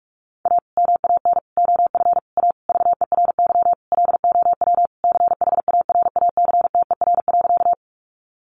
Morse code, heard as a keyed tone, about 44 wpm, 700 Hz. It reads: A MAN OF A VERY LOW CHARACTER.
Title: Sentences from Top 1200 Words 45wpm